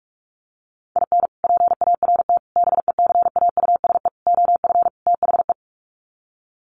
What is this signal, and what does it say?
Morse code, heard as a keyed tone, about 44 wpm, 700 Hz. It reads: IN PART BECAUSE OF THE
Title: Sets of 5 Words Encore2 45wpm